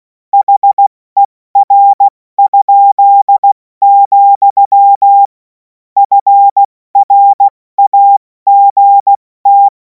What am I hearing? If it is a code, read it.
Morse code, heard as a keyed tone, about 16 wpm, 800 Hz. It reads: HER?, FRAGT
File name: Morsekurs / Witze / Witz 07. April 2026 70 F80